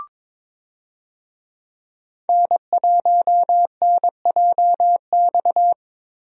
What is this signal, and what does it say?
Morse code, heard as a keyed tone, about 22 wpm, 700 Hz. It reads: N1NJX
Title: USA Call Signs - 5 Characters - Repeated 2x - 22wpm